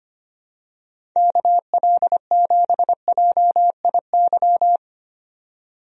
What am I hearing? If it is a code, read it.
Morse code, heard as a keyed tone, about 25 wpm, 700 Hz. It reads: KL7JIY
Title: USA Call Signs - 6 Characters - 25wpm